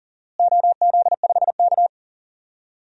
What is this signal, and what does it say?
Morse code, heard as a keyed tone, about 40 wpm, 700 Hz. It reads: OZ5K
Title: DXCC-100 Call Signs - 4 Characters - Repeated 2x - 40wpm